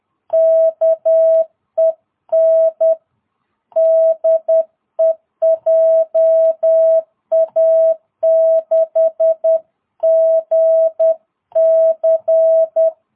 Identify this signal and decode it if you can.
Morse code, heard as a keyed tone, about 10 wpm, 660 Hz. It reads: KEN DE JA6GC
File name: jh8cbh / 2022cw